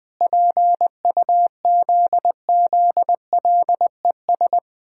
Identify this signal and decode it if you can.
Morse code, heard as a keyed tone, about 19 wpm, 700 Hz. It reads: PUZZLES
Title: Pangram Challenge v2 - 20wpm